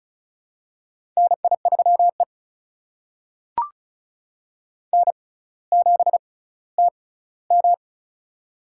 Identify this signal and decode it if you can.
Morse code, heard as a keyed tone, about 36 wpm, spaced as for 18 wpm, 700 Hz. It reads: NI3E N7TM
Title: USA Call Signs - 4 Characters - 6x Character Spacing - 35wpm